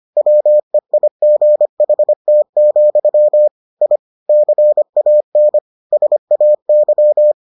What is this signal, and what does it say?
Morse code, heard as a keyed tone, about 25 wpm, 600 Hz. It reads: WEIGHT, I CAN SAY